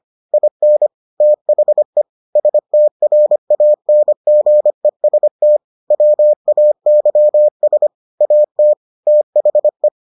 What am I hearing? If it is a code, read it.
Morse code, heard as a keyed tone, about 25 wpm, 600 Hz. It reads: IN THE STRANGEST WAYS AT THE